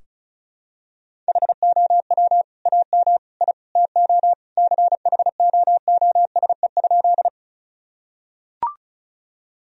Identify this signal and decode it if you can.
Morse code, heard as a keyed tone, about 36 wpm, 700 Hz. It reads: HOW AM I TO CHOOSE?
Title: Sentences from Top 1000 Words - Review Part 4 - 35wpm